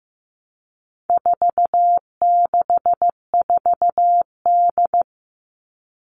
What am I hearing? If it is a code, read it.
Morse code, heard as a keyed tone, about 15 wpm, 700 Hz. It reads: 464D
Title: All Cut Numbers - 4-Digit 15wpm